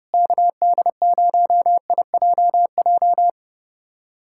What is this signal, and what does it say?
Morse code, heard as a keyed tone, about 30 wpm, 700 Hz. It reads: KD0IJJ
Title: USA Callsign DE Callsign - 6 Characters - 30wpm